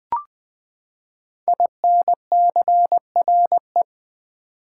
Morse code, heard as a keyed tone, about 20 wpm, 700 Hz. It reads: INCRE
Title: Build A Word - 5 Letters At A Time - 12-Letter Words - 20wpm